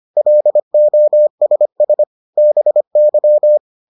Morse code, heard as a keyed tone, about 25 wpm, 600 Hz. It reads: LOSS BY